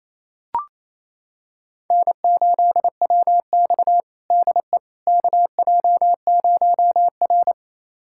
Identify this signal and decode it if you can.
Morse code, heard as a keyed tone, about 28 wpm, 700 Hz. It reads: N8WX DE KJ0R
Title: USA Callsign DE Callsign - 4 Characters - 28wpm